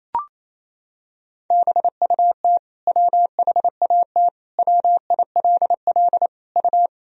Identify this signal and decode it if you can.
Morse code, heard as a keyed tone, about 28 wpm, 700 Hz. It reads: BUT WHAT WILL U
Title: Sentences from Top 100 Words and Abbreviations 28wpm